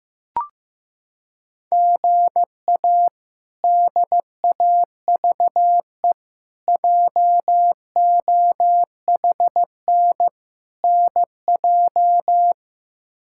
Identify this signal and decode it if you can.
Morse code, heard as a keyed tone, about 15 wpm, 700 Hz. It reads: GA DAVE JOHN NJ
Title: K1USN SST QSO Practice - 15wpm